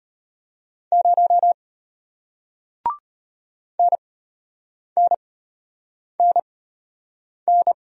Morse code, heard as a keyed tone, about 33 wpm, spaced as for 8 wpm, 700 Hz. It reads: 0 NNNN